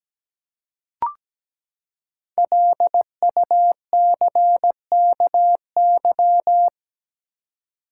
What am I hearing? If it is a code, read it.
Morse code, heard as a keyed tone, about 17 wpm, 700 Hz. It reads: LUCKY
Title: Five Letter Words - 17wpm